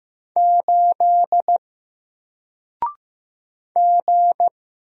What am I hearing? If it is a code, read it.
Morse code, heard as a keyed tone, about 15 wpm, 700 Hz. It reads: 8 G
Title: Single Letter-Number Alt 15wpm